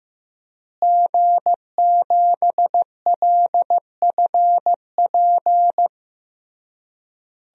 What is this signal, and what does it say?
Morse code, heard as a keyed tone, about 15 wpm, 700 Hz. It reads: G7LFP
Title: DXCC-100 Call Signs - 5 Characters - Repeated 3x - 15wpm